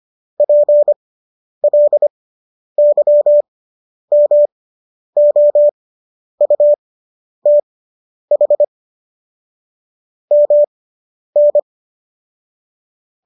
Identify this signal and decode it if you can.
Morse code, heard as a keyed tone, about 25 wpm, spaced as for 10 wpm, 590 Hz. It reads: PLYMOUTH MN